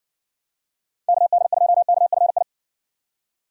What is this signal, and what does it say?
Morse code, heard as a keyed tone, about 58 wpm, 700 Hz. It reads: KD2KFI